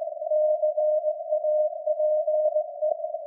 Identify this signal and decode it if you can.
Morse code, heard as a keyed tone, about 17 wpm, 610 Hz. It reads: CAPE
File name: iz7slz / VLF / SAQ jun 2019